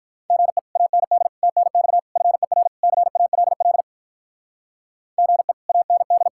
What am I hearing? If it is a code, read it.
Morse code, heard as a keyed tone, about 46 wpm, 700 Hz. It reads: GE AND TNX FER CALL GE AND